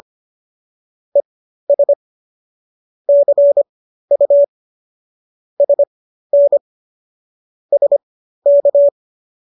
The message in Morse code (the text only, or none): ES CU SN SK